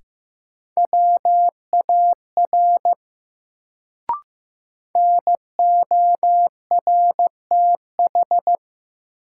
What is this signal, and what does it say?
Morse code, heard as a keyed tone, about 15 wpm, 700 Hz. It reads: WAR NORTH
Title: Sentences from Top 400 Words - New Word Drill - 15wpm